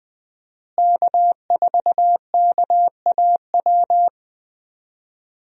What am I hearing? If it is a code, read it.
Morse code, heard as a keyed tone, about 20 wpm, 700 Hz. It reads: K4KAW
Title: USA Call Signs - 5 Characters - Repeated 2x - 20wpm